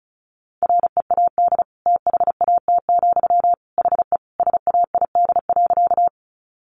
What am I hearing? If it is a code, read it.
Morse code, heard as a keyed tone, about 34 wpm, 700 Hz. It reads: READ THAT, HE SAID.
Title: Sentences from Top 200 Words 35wpm